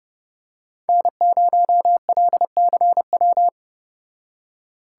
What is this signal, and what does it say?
Morse code, heard as a keyed tone, about 29 wpm, 700 Hz. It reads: N0LCW